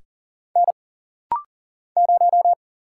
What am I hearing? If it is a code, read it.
Morse code, heard as a keyed tone, about 41 wpm, 700 Hz. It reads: N 0
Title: Single Letter-Number - Rapid-Fire 40wpm